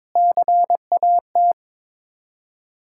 Morse code, heard as a keyed tone, about 22 wpm, 700 Hz. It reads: CAT
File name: Sets of 9 Words - 22wpm